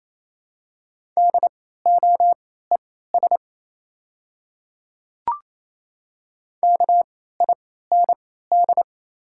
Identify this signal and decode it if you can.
Morse code, heard as a keyed tone, about 27 wpm, spaced as for 16 wpm, 700 Hz. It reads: DOES KIND
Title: 3x Character Spacing - Top 200 Words - 28wpm